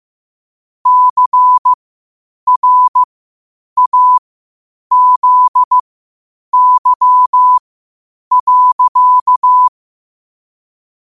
Morse code, heard as a keyed tone, about 15 wpm, spaced as for 8 wpm, 1000 Hz. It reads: CRAZY.
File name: qotdfiles / 8wpm